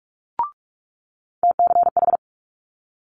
Morse code, heard as a keyed tone, about 46 wpm, 700 Hz. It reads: TCH
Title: Multi-letter Phonemes Quick Fire 45wpm